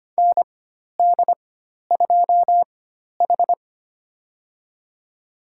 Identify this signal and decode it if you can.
Morse code, heard as a keyed tone, about 25 wpm, spaced as for 12 wpm, 700 Hz. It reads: ND2H
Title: USA Call Signs - 4 Characters - 4x Character Spacing - 25wpm